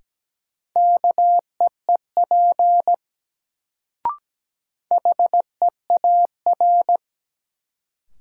Morse code, heard as a keyed tone, about 17 wpm, 700 Hz. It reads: KEEP HEAR
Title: Sentences from Top 300 Words - New Word Drill - 17wpm